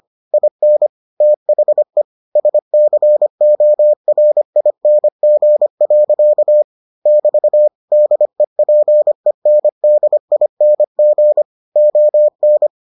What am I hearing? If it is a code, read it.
Morse code, heard as a keyed tone, about 25 wpm, 600 Hz. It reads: IN THE SCORING. <BT> DEPENDING ON